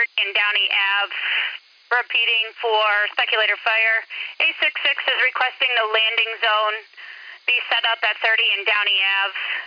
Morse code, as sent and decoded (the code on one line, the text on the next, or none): none